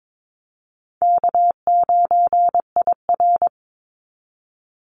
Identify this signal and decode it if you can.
Morse code, heard as a keyed tone, about 22 wpm, 700 Hz. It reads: K9IR
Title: Easy Call Signs Repeated 2x - 22wpm